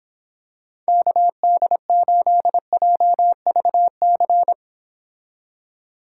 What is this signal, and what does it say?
Morse code, heard as a keyed tone, about 26 wpm, 700 Hz. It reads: KD8JVC